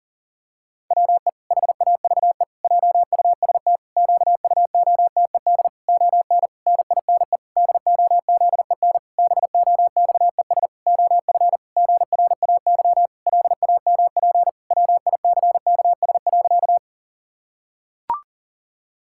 Morse code, heard as a keyed tone, about 40 wpm, 700 Hz. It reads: WE HAVE JUST QUOTED ON NINE DOZEN BOXES OF GRAY LAMP WICKS.
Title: Pangram Challenge v2 - 40wpm